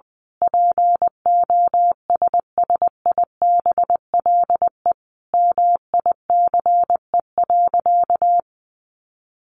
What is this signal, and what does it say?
Morse code, heard as a keyed tone, about 20 wpm, 700 Hz. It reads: POSSIBLE MICE.
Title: A Sound Of Thunder 20wpm